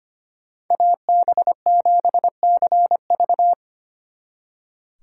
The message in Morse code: AB7CV

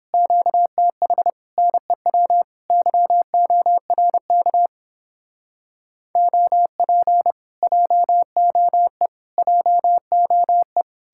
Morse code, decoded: QTH NEW YORK OP JOE JOE

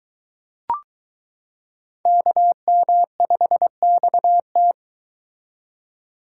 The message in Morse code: KM5XT

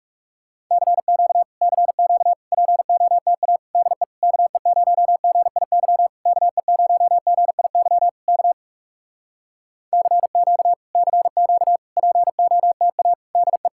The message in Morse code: CQ CQ POTA DE KE0GIY KE0GIY K CQ CQ POTA DE

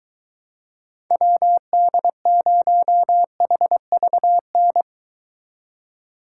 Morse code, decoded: WD0HVN

